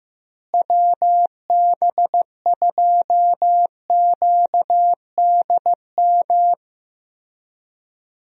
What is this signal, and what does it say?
Morse code, heard as a keyed tone, about 15 wpm, 700 Hz. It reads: WB2QDM
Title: USA Call Signs - 6 Characters - Repeated 3x - 15wpm